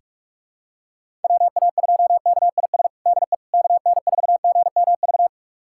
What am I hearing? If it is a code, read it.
Morse code, heard as a keyed tone, about 46 wpm, 690 Hz. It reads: WA2KIS DE KN4GMV
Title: USA Callsign DE Callsign 45wpm